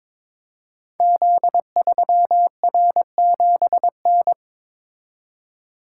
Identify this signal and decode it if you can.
Morse code, heard as a keyed tone, about 22 wpm, 700 Hz. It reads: Z3R7N
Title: Code Groups - 5 Letter:Numbers - 22wpm